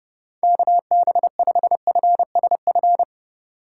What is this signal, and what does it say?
Morse code, heard as a keyed tone, about 30 wpm, 700 Hz. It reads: KB5FSF